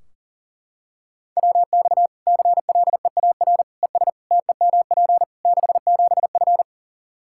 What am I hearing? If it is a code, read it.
Morse code, heard as a keyed tone, about 40 wpm, 700 Hz. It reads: WX CLEAR ES TEMP 67F
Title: K1USN SST QSO Practice 40wpm